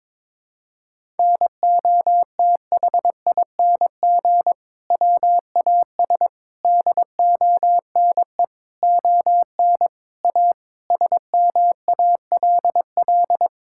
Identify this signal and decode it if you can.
Morse code, heard as a keyed tone, about 22 wpm, 700 Hz. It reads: NOTHING WAS DONE ON A SMALL
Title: Sentences from Top 1000 Words - Review Part 5 - 22wpm